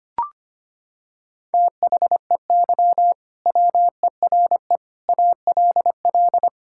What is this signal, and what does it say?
Morse code, heard as a keyed tone, about 25 wpm, 700 Hz. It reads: THEY WERE ALL
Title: Sentences from Top 1000 Words - Review Part 5 - 25wpm